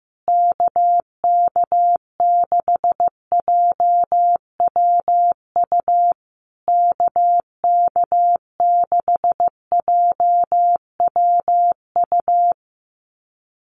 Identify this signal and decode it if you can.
Morse code, heard as a keyed tone, about 15 wpm, 700 Hz. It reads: KK6JWU KK6JWU
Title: USA Call Signs Repeated 3x 15wpm